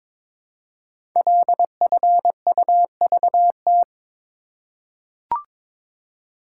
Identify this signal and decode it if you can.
Morse code, heard as a keyed tone, about 22 wpm, 700 Hz. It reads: LFUVT
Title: Code Groups - 5 Letters - 22wpm